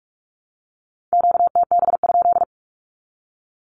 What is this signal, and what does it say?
Morse code, heard as a keyed tone, about 44 wpm, 700 Hz. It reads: QTB?
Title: Q-Codes 45wpm